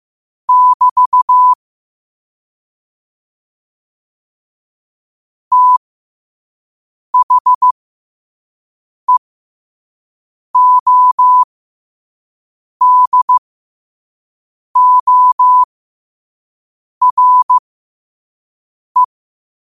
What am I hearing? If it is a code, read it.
Morse code, heard as a keyed tone, about 15 wpm, spaced as for 5 wpm, 1000 Hz. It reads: <BT> THEODORE
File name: 5wpm